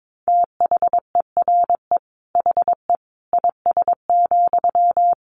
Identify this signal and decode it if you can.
Morse code, heard as a keyed tone, about 22 wpm, 700 Hz. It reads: THERE HE IS,
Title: Sentences from Top 100 Words 22wpm